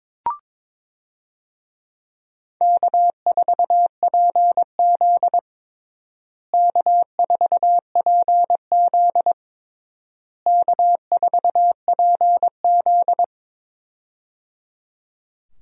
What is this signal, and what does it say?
Morse code, heard as a keyed tone, about 23 wpm, 700 Hz. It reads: K4PZ K4PZ K4PZ
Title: USA Call Signs - 4 Characters - Repeated 3x - 22wpm